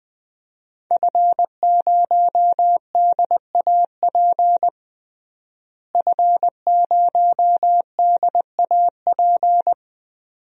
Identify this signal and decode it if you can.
Morse code, heard as a keyed tone, about 20 wpm, 700 Hz. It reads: F0DAP F0DAP
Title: DXCC-100 Call Signs - 5 Characters - Repeated 2x - 20wpm